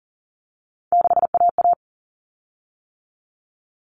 Morse code, heard as a keyed tone, about 40 wpm, 700 Hz. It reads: 6AA
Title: Common Cut Numbers - 3-Digit 40wpm